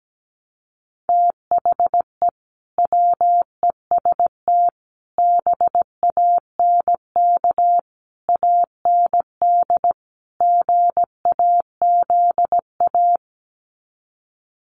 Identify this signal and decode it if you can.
Morse code, heard as a keyed tone, about 17 wpm, 700 Hz. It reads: THE WEST BANK AND GAZA
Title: Sets of 5 Words 17wpm